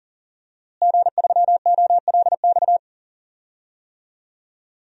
Morse code, heard as a keyed tone, about 40 wpm, 700 Hz. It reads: G3OLX
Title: DXCC-100 Call Signs - 5 Characters - Repeated 3x - 40wpm